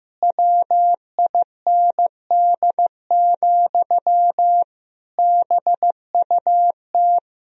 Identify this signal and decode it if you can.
Morse code, heard as a keyed tone, about 15 wpm, 700 Hz. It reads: WIND, BUT
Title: Sentences from Top 1000 Words - Review Part 3 - 15wpm